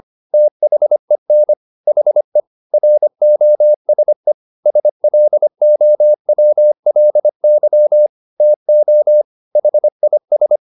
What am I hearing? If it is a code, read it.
Morse code, heard as a keyed tone, about 25 wpm, 600 Hz. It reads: THEN HE ROSE SLOWLY TO HIS